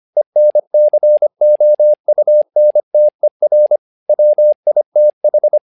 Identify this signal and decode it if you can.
Morse code, heard as a keyed tone, about 25 wpm, 600 Hz. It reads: ENCOUNTER WITH
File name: learning center / tower trouble / tower trouble 25wpm